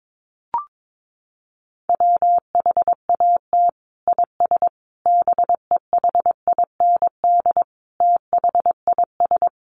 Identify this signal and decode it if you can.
Morse code, heard as a keyed tone, about 22 wpm, 700 Hz. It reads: WHAT IS BEHIND THIS
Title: Sentences from Top 1100 Words 22wpm